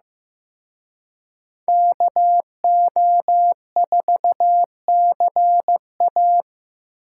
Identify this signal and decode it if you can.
Morse code, heard as a keyed tone, about 18 wpm, 700 Hz. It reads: KO4CA
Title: Speed Racing - USA Call Signs 15wpm